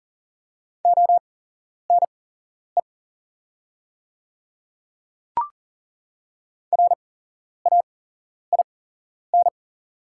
Morse code, heard as a keyed tone, about 39 wpm, spaced as for 11 wpm, 690 Hz. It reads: ONE RAIN
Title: 8x Character Spacing - Top 300 Words - 40wpm